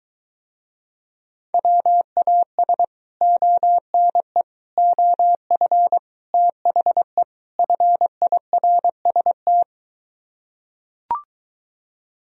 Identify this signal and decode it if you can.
Morse code, heard as a keyed tone, about 23 wpm, 700 Hz. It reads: WAS ONE OF THE FIRST